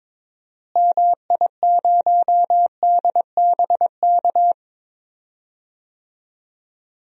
Morse code, heard as a keyed tone, about 21 wpm, 700 Hz. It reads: MI0DBK
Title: DXCC-100 Call Signs - 6 Characters - Repeated 2x - 22wpm